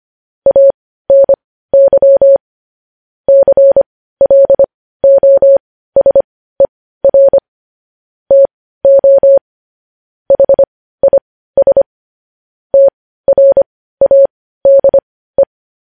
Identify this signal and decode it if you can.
Morse code, heard as a keyed tone, about 25 wpm, spaced as for 15 wpm, 550 Hz. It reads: ANY CLOSER TO HIS TRADE
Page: news-headlines-in-morse-code-at-15-wpm